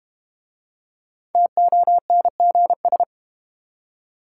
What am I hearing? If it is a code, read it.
Morse code, heard as a keyed tone, about 30 wpm, 700 Hz. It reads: TONGS
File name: Easy Words - Speed-Racing with Hints - 25wpm